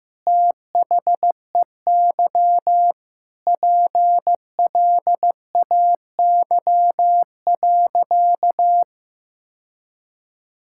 Morse code, heard as a keyed tone, about 16 wpm, 700 Hz. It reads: THEY PLAY.
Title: Sentences from Top 1000 Words - Review Part 5 - 15wpm